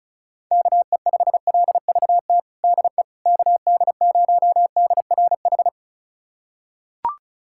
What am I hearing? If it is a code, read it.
Morse code, heard as a keyed tone, about 36 wpm, 700 Hz. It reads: KE5LVT DE KD0DRH
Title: USA Callsign DE Callsign - 6 Characters - 35wpm